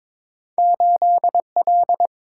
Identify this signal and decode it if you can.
Morse code, heard as a keyed tone, about 23 wpm, 700 Hz. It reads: 8L
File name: practice / 22wpm / Two Letter-Number Combinations - 22wpm